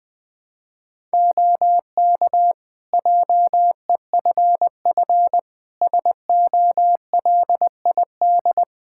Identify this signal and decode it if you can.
Morse code, heard as a keyed tone, about 20 wpm, 700 Hz. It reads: OK JEFF SOLID
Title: K1USN SST QSO Practice 20wpm